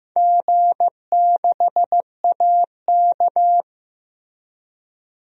Morse code, heard as a keyed tone, about 15 wpm, 700 Hz. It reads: G6AK
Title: DXCC-100 Call Signs - 4 Characters - Repeated 2x - 15wpm